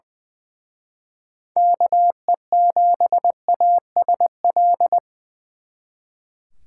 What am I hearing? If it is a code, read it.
Morse code, heard as a keyed tone, about 24 wpm, 700 Hz. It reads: KE7ASL
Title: Speed Racing - USA 2x3 Call Signs 20wpm